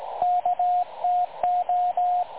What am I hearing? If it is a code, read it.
Morse code, heard as a keyed tone, about 19 wpm, 720 Hz, with radio noise. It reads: KTO
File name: dlazaawansowanych / nagraniazpasma / websdr recording 3565.0kHz